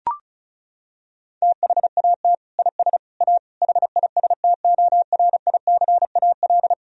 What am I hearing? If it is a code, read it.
Morse code, heard as a keyed tone, about 36 wpm, 700 Hz. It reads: THAT IS A HISTORICAL
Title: Sentences from Top 1000 Words - Review Part 4 - 35wpm